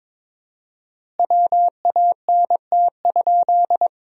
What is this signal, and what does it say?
Morse code, heard as a keyed tone, about 21 wpm, 700 Hz. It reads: WANT?
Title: 4x Word-Spacing - Sentences from Top 200 Words - 22wpm